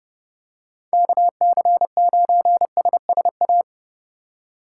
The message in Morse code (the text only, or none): KC9SSA